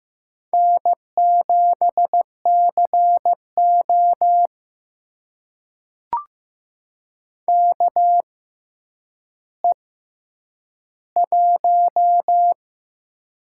Morse code, heard as a keyed tone, about 15 wpm, spaced as for 9 wpm, 700 Hz. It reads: N7CO KE1
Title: USA Call Signs - 4 Characters - 6x Character Spacing - 15wpm